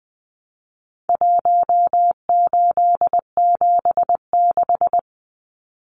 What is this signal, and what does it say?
Morse code, heard as a keyed tone, about 20 wpm, 700 Hz. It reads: 1876
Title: Numbers - Contest Style - 4-Digit 20wpm